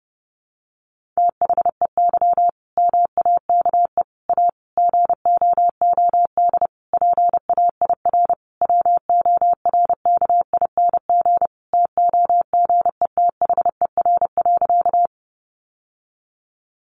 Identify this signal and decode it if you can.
Morse code, heard as a keyed tone, about 28 wpm, 700 Hz. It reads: THEY MAKE A GOOD PAIR WORKING TOGETHER.
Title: Sentences from Top 1200 Words 30wpm